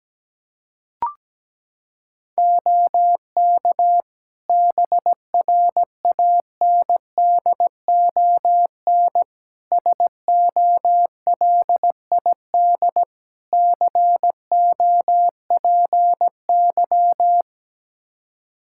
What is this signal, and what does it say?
Morse code, heard as a keyed tone, about 17 wpm, 700 Hz. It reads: OK BRANDON SOLID COPY
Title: K1USN SST QSO Practice 17wpm